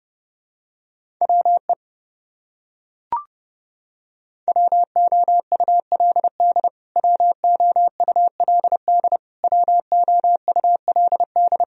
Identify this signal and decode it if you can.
Morse code, heard as a keyed tone, about 30 wpm, 700 Hz. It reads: WE WOULD WOULD WOULD
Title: Top 100 Words Repeated 30wpm